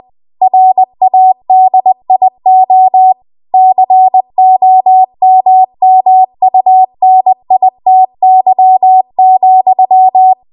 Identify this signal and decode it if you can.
Morse code, heard as a keyed tone, about 20 wpm, 750 Hz. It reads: RADIO COMMUNITY,
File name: Morse / 20WPM